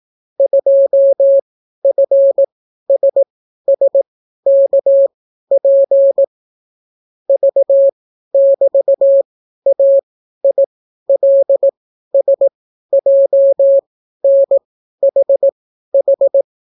Morse code, heard as a keyed tone, about 18 wpm, spaced as for 12 wpm, 550 Hz. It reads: 2FSSKP V<BT>AILSJNHH